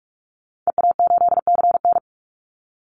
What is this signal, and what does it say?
Morse code, heard as a keyed tone, about 44 wpm, 700 Hz. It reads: EA8CN
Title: Call Signs Repeated 2x - 45wpm